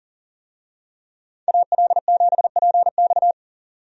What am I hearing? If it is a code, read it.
Morse code, heard as a keyed tone, about 40 wpm, 700 Hz. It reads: AL7PX
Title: USA Call Signs - 5 Characters - 40wpm